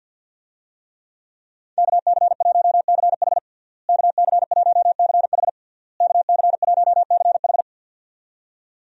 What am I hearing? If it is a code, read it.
Morse code, heard as a keyed tone, about 50 wpm, 700 Hz. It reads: KC1CH KC1CH KC1CH
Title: USA Call Signs - 5 Characters - Repeated 3x - 50wpm